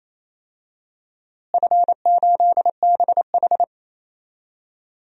Morse code, heard as a keyed tone, about 28 wpm, 700 Hz. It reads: F8BH
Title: DXCC-100 Call Signs - 4 Characters - Repeated 3x - 28wpm